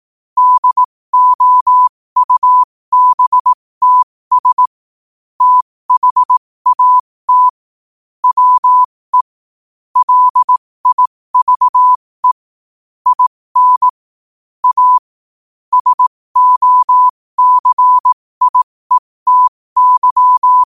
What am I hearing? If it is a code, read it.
Morse code, heard as a keyed tone, about 18 wpm, spaced as for 15 wpm, 1000 Hz. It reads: DOUBTS THAT WE LIVE IN A SOCIETY
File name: qotdfiles / 15wpm